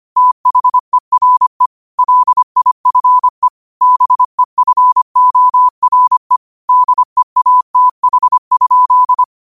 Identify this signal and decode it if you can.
Morse code, heard as a keyed tone, about 25 wpm, 1000 Hz. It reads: THERE LIFE BEFORE DEATH?
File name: qotdfiles / 25wpm